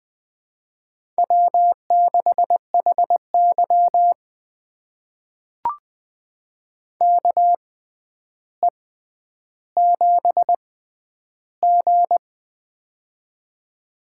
Morse code, heard as a keyed tone, about 20 wpm, spaced as for 10 wpm, 700 Hz. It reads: W6HY KE7G
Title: USA Call Signs - 4 Characters - 6x Character Spacing - 20wpm